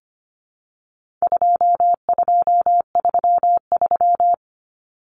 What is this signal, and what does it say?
Morse code, heard as a keyed tone, about 25 wpm, 700 Hz. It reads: 2233
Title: Common Cut Numbers - 4-Digit 25wpm